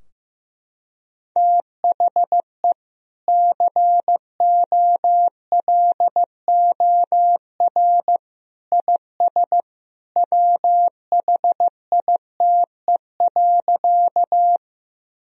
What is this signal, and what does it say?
Morse code, heard as a keyed tone, about 15 wpm, 700 Hz. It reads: THE COLOR IS WHITE.
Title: Sentences from Top 1000 Words - Review Part 2 - 15wpm